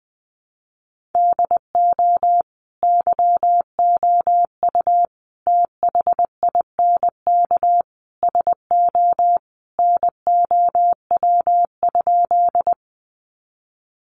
Morse code, heard as a keyed tone, about 20 wpm, 700 Hz. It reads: DO YOU THINK SO NOW?
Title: Sentences from Top 200 Words 20wpm